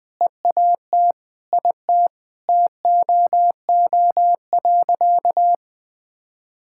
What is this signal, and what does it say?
Morse code, heard as a keyed tone, about 20 wpm, 700 Hz. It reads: EAT IT TOO.
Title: Sentences with 9-Word Phrases - 20wpm